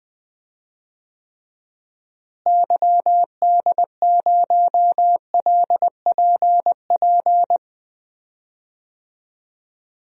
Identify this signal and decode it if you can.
Morse code, heard as a keyed tone, about 20 wpm, 700 Hz. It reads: YD0LPP
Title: DXCC-100 Call Signs - 6 Characters - Repeated 2x - 20wpm